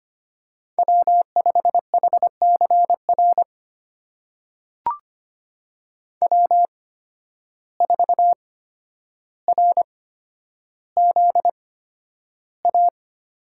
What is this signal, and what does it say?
Morse code, heard as a keyed tone, about 25 wpm, spaced as for 11 wpm, 700 Hz. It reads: W5HCR W4RZA
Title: USA Call Signs - 5 Characters - 8x Character Spacing - 25wpm